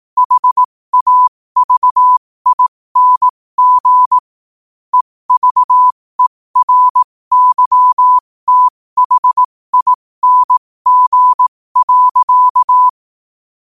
Morse code, heard as a keyed tone, about 18 wpm, spaced as for 15 wpm, 1000 Hz. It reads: HAVING EVERYTHING.
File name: qotdfiles / 15wpm